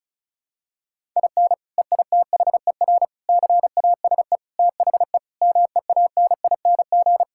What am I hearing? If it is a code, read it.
Morse code, heard as a keyed tone, about 36 wpm, 700 Hz. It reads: IN EITHER CASE THE MEANING
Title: Sentences from Top 1000 Words - Review Part 6 - 35wpm